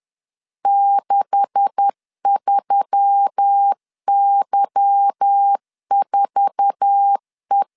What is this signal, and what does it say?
Morse code, heard as a keyed tone, about 10 wpm, 800 Hz. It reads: 63Y4E
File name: CW / DET / 1 9 1500C S A 10wpm 4511